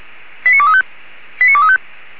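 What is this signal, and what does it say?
Five-tone selective call (ZVEI), heard as quick runs of five tones, tones of about 70 ms. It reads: 89235 89235
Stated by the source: ZVEI five-tone selective call received by SDR.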